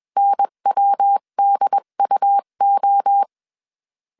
Morse code, heard as a keyed tone, about 21 wpm, 800 Hz. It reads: NWDUO
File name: 20wpm